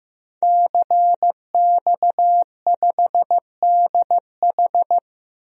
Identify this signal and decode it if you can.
Morse code, heard as a keyed tone, about 16 wpm, 700 Hz. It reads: CX5DH
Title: DXCC-100 Call Signs - 5 Characters - 15wpm